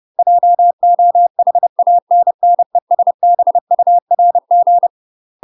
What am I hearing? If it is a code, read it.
Morse code, heard as a keyed tone, about 30 wpm, 700 Hz. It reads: JOHANNESBURG
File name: UK MASTER QSOs TRAINING / 30WPM / UK MASTER QSOs 0020